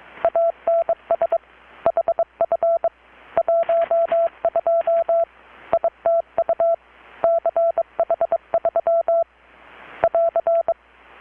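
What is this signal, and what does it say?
Morse code, heard as a keyed tone, about 22 wpm, 670 Hz. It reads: ANS HF 12 ITU CH3 <AR>
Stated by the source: Morse from KPH's Night of Nights XX maritime radiotelegraphy event, received in Chiapas, Mexico about 3,800 km away on a Kenwood R-600 with a grounded 10 m long-wire antenna.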